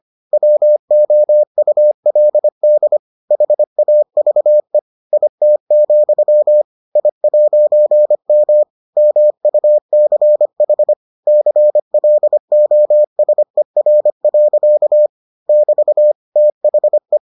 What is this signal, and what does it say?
Morse code, heard as a keyed tone, about 25 wpm, 600 Hz. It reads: WOULD HAVE IT, I'M MUCH CLOSER. <BT> THE